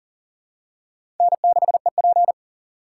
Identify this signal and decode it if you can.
Morse code, heard as a keyed tone, about 40 wpm, 700 Hz. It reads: N6EP
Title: USA Call Signs - 2x Character Spacing - 40wpm